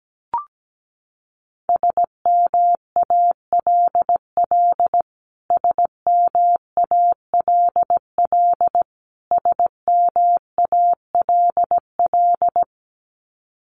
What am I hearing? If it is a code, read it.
Morse code, heard as a keyed tone, about 17 wpm, 700 Hz. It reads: SMALL SMALL SMALL
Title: Top 500 Words Repeated 17wpm